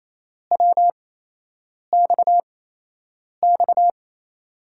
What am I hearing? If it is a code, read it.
Morse code, heard as a keyed tone, about 28 wpm, spaced as for 8 wpm, 700 Hz. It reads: WXX
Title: USA Call Signs - 5 Characters - 8x Character Spacing - 28wpm